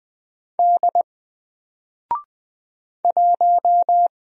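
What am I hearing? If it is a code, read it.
Morse code, heard as a keyed tone, about 20 wpm, 700 Hz. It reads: D 1
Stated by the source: Single letter and number Morse drill at 20 wpm, from N4LCD.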